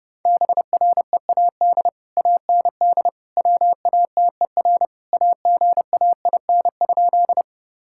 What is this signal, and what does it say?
Morse code, heard as a keyed tone, about 32 wpm, 700 Hz. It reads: BREAD AND WATER AGAIN?
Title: Sentences from Top 1600 Words 30wpm